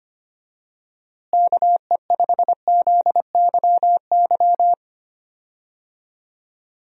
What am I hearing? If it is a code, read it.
Morse code, heard as a keyed tone, about 25 wpm, 700 Hz. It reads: KE5ZYY
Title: USA Call Signs - 6 Characters - Repeated 2x - 25wpm